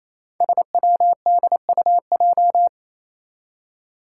Morse code, heard as a keyed tone, about 28 wpm, 700 Hz. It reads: SWDUJ
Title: Five Challenging Letter-Number Combinations 28wpm